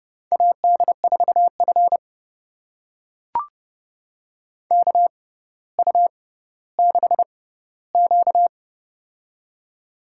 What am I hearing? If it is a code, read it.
Morse code, heard as a keyed tone, about 30 wpm, spaced as for 15 wpm, 700 Hz. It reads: AD4F KU6Q
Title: USA Call Signs - 4 Characters - 6x Character Spacing - 30wpm